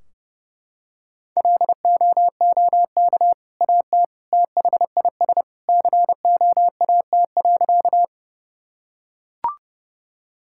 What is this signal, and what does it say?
Morse code, heard as a keyed tone, about 32 wpm, 700 Hz. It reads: LOOK AT THIS COAT.
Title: Sentences from Top 1600 Words 30wpm